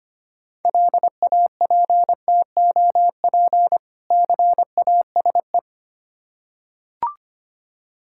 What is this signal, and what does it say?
Morse code, heard as a keyed tone, about 25 wpm, 700 Hz. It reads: LAPTOP CASE
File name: practice / 25wpm / Difficult Words with Hints - 25wpm